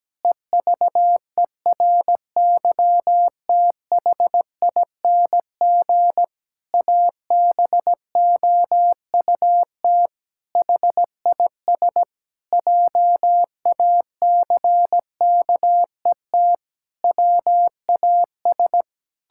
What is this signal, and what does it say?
Morse code, heard as a keyed tone, about 17 wpm, 700 Hz. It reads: EVERYTHING ABOUT HIS JACKET WAS
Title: Pangram Challenge v2 - 17wpm